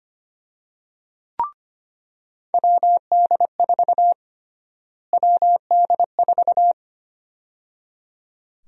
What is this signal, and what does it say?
Morse code, heard as a keyed tone, about 25 wpm, 700 Hz. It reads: WD4 WD4
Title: USA Call Sign Prefixes - Repeated 2x - 25wpm